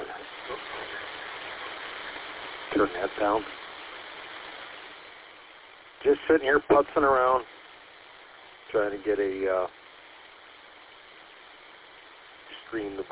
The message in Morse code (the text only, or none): none